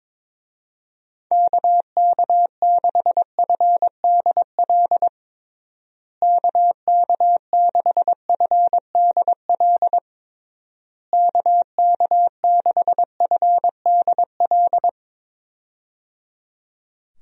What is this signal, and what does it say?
Morse code, heard as a keyed tone, about 22 wpm, 700 Hz. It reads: KK6FDL KK6FDL KK6FDL
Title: USA Call Signs - 6 Characters - Repeated 3x - 22wpm